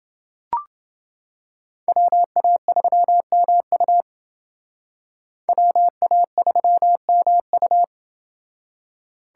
Morse code, heard as a keyed tone, about 28 wpm, 700 Hz. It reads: WA3MU WA3MU